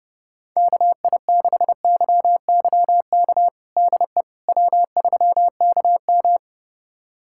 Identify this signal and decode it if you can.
Morse code, heard as a keyed tone, about 30 wpm, 700 Hz. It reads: KI6YYK DE W3KM
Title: USA Callsign DE Callsign 30wpm